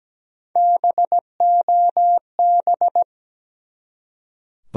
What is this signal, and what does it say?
Morse code, heard as a keyed tone, about 17 wpm, 700 Hz. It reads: BOB
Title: USA Common Names - Max 3 Letters - 17wpm